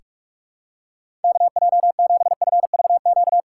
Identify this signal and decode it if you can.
Morse code, heard as a keyed tone, about 46 wpm, 700 Hz. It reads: KJ7FVQ
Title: USA Call Signs - 6 Characters - 8x Character Spacing - 45wpm